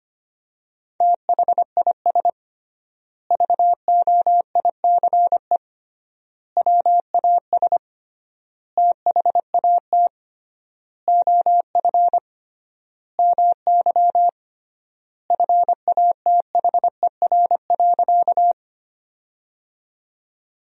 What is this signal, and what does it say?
Morse code, heard as a keyed tone, about 25 wpm, 700 Hz. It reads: THIS VOICE WAS THAT OF MY FATHER.
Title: 3x Word-Spacing - Sentences from Top 500 Words - 25wpm